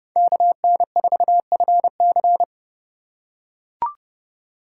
Morse code, heard as a keyed tone, about 30 wpm, 700 Hz. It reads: KN4FC
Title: USA Call Signs - 5 Characters - 3x Character Spacing - 30wpm